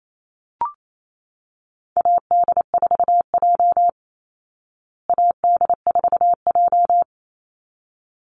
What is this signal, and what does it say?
Morse code, heard as a keyed tone, about 28 wpm, 700 Hz. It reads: AD4J AD4J
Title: Call Signs Repeated 3x - 28wpm